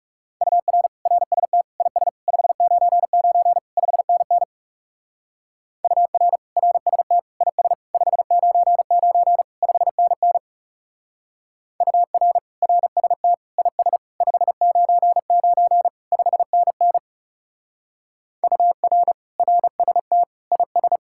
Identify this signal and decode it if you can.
Morse code, heard as a keyed tone, about 37 wpm, 700 Hz. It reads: UR RST IS 599 5NN UR RST IS 599 5NN UR RST IS 599 5NN UR RST IS
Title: Speed-Racing Common QSO Phrases Encore - 30wpm